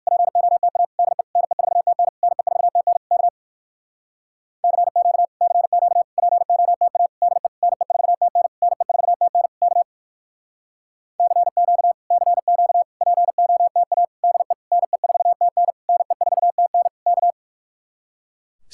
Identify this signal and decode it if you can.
Morse code, heard as a keyed tone, about 58 wpm, 700 Hz. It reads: POTA DE NE4TN NE4TN K CQ CQ POTA DE NE4TN NE4TN K CQ CQ POTA DE NE4TN NE4TN K